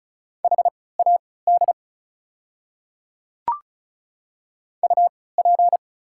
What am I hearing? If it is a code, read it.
Morse code, heard as a keyed tone, about 36 wpm, spaced as for 20 wpm, 690 Hz. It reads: HAD UP
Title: 3x Character Spacing - Top 100 Words - 35wpm